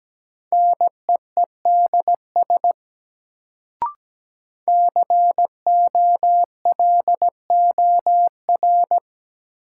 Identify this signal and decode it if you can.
Morse code, heard as a keyed tone, about 17 wpm, 700 Hz. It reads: NEEDS COLOR
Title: Sentences from Top 1600 Words - New Word Drill Alt - 17wpm